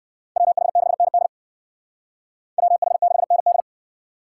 Morse code, heard as a keyed tone, about 58 wpm, 700 Hz. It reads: WH6ND WH6ND